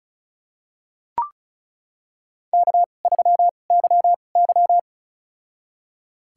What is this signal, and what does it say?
Morse code, heard as a keyed tone, about 36 wpm, spaced as for 25 wpm, 700 Hz. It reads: K3YY
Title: USA Call Signs - 4 Characters - 2x Character Spacing - 35wpm